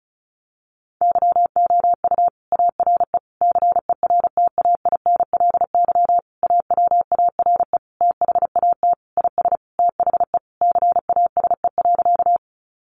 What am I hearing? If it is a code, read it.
Morse code, heard as a keyed tone, about 34 wpm, 700 Hz. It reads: YOU ARE CERTAINLY AWARE THAT IS THE CASE.
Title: Sentences from Top 1200 Words 35wpm